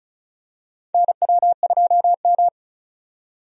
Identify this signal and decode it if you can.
Morse code, heard as a keyed tone, about 34 wpm, 700 Hz. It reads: NW2M
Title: USA Call Signs - 4 Characters - 35wpm